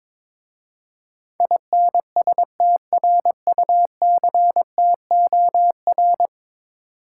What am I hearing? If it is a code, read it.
Morse code, heard as a keyed tone, about 22 wpm, 700 Hz. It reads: INSTRUCTOR
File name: practice / 22wpm / 10 Letter Words - First 3 Letters Given - 22wpm